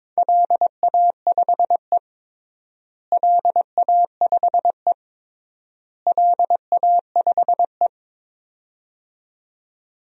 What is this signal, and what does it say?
Morse code, heard as a keyed tone, about 22 wpm, 700 Hz. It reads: LA5E LA5E LA5E
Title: DXCC-100 Call Signs - 4 Characters - Repeated 3x - 22wpm